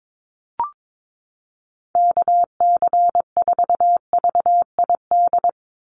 Morse code, heard as a keyed tone, about 22 wpm, 700 Hz. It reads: KC4VID
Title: USA Call Signs 22wpm